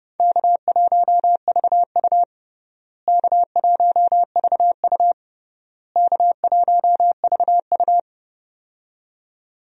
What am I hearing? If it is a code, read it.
Morse code, heard as a keyed tone, about 31 wpm, 700 Hz. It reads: K1VU K1VU K1VU
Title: USA Call Signs - 4 Characters - Repeated 3x - 30wpm